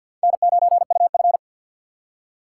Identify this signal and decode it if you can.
Morse code, heard as a keyed tone, about 53 wpm, 700 Hz. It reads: N9UF
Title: USA Call Signs - 6x Character Spacing - 50wpm